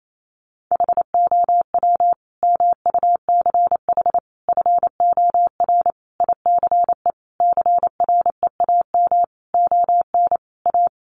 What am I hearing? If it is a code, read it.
Morse code, heard as a keyed tone, about 28 wpm, 700 Hz. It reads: HOW MUCH FOR ICE CREAM ON A
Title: Sentences from Top 1500 Words 28wpm